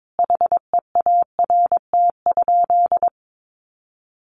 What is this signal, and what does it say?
Morse code, heard as a keyed tone, about 22 wpm, 700 Hz. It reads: HEART?
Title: Sentences from Top 1200 Words 22wpm